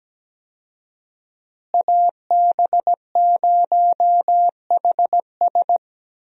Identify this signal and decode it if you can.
Morse code, heard as a keyed tone, about 17 wpm, 700 Hz. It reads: AB0HS
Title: USA Call Signs - 5 Characters - Repeated 2x - 17wpm